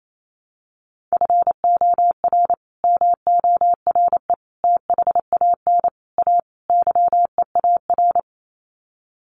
Morse code, heard as a keyed tone, about 28 wpm, 700 Hz. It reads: FOR MORE THAN A YEAR
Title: Sets of 5 Words 28wpm